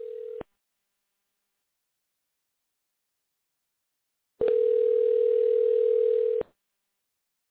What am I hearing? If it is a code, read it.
Ringback tone (North America), 440 + 480 Hz.